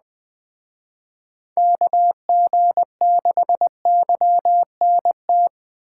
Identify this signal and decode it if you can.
Morse code, heard as a keyed tone, about 20 wpm, 700 Hz. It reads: KG6YNT
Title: Speed Racing - USA Call Signs 17wpm